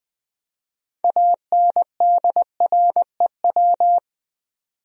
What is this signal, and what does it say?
Morse code, heard as a keyed tone, about 20 wpm, 700 Hz. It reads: ANDREW
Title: USA Common Names - 20wpm